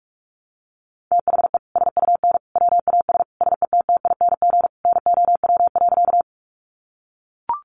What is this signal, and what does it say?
Morse code, heard as a keyed tone, about 44 wpm, 700 Hz. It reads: THE SUN WAS SETTING NOW.
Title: Sentences from Top 1300 Words 45wpm